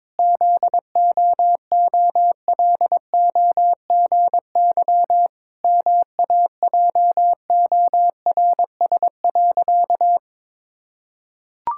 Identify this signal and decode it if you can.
Morse code, heard as a keyed tone, about 21 wpm, 700 Hz. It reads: ZOOLOGY MAJORS.
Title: Pangram Challenge v2 - 22wpm